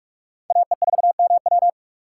Morse code, heard as a keyed tone, about 55 wpm, 700 Hz. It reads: AE4MW
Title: Speed Racing - USA Call Signs 45wpm